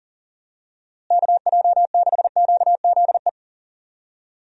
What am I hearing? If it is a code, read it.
Morse code, heard as a keyed tone, about 40 wpm, 700 Hz. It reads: KJ6QZE
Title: USA Call Signs - 6 Characters - Repeated 2x - 40wpm